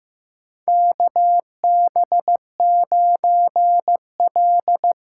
Morse code, heard as a keyed tone, about 15 wpm, 700 Hz. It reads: KB9L
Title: USA Call Signs - 6 Characters - 8x Character Spacing - 15wpm